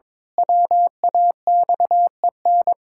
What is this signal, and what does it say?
Morse code, heard as a keyed tone, about 21 wpm, 700 Hz. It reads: WAXEN
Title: Pangram Challenge v2 - 22wpm